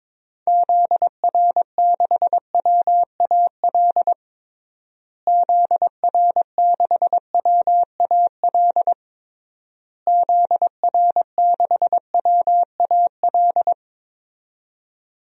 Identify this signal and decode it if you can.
Morse code, heard as a keyed tone, about 22 wpm, 700 Hz. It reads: ZR6WAL ZR6WAL ZR6WAL
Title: DXCC-100 Call Signs - 6 Characters - Repeated 3x - 22wpm